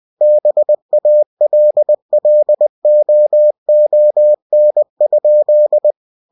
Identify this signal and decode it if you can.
Morse code, heard as a keyed tone, about 20 wpm, 600 Hz. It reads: BALLOON?
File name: cw-books / winnie-the-pooh / winnie-20wpm-page-0053